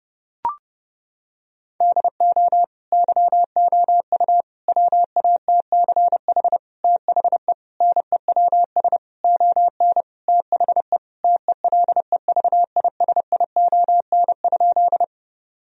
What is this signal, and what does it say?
Morse code, heard as a keyed tone, about 32 wpm, 700 Hz. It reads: DO YOU WATCH THE NEWS ON THE TELEVISION?
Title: Sentences from Top 1000 Words - Review Part 4 - 30wpm